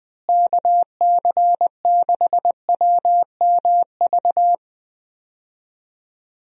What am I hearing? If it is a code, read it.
Morse code, heard as a keyed tone, about 19 wpm, 700 Hz. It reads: KC6WMV
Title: USA Call Signs - 6 Characters - Repeated 3x - 20wpm